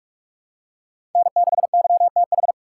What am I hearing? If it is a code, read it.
Morse code, heard as a keyed tone, about 42 wpm, 700 Hz. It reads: N6YTH
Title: USA Call Signs - 5 Characters - 6x Character Spacing - 45wpm